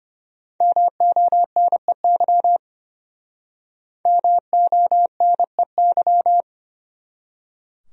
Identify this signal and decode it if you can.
Morse code, heard as a keyed tone, about 29 wpm, 700 Hz. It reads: MONEY MONEY
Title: Speed-Racing Top 500 Words 25wpm